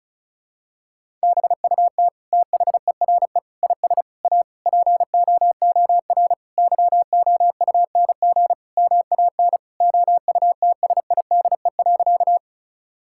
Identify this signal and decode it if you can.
Morse code, heard as a keyed tone, about 36 wpm, 700 Hz. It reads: BUT THERE IS A POOR YOUNG MAN OUTSIDE.
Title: Sentences from Top 1000 Words - Review Part 6 - 35wpm